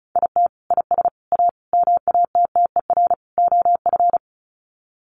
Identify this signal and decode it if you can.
Morse code, heard as a keyed tone, about 34 wpm, 700 Hz. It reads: IT IS A MATTER OF
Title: Sets of 5 Words Encore2 35wpm